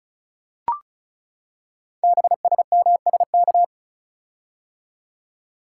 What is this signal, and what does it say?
Morse code, heard as a keyed tone, about 36 wpm, 700 Hz. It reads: BSMSK
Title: Code Groups - 5 Letters - 35wpm